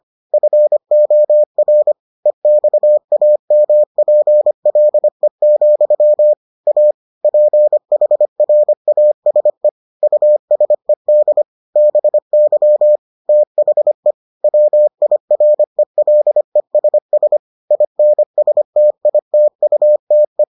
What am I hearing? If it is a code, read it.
Morse code, heard as a keyed tone, about 25 wpm, 600 Hz. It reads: FOR EXAMPLE, A PHRASE USED BY THE WIRELESS INSTITUTE